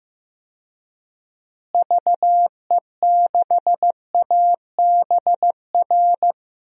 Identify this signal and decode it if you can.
Morse code, heard as a keyed tone, about 15 wpm, 700 Hz. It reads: VE6ABR
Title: DXCC-100 Call Signs - 6 Characters - Repeated 2x - 15wpm